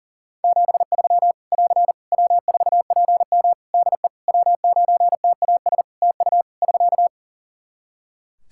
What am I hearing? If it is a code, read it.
Morse code, heard as a keyed tone, about 37 wpm, 700 Hz. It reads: 73 <AR> W4PM DE W9TAS TU <SK>
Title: K1USN SST QSO Practice 40wpm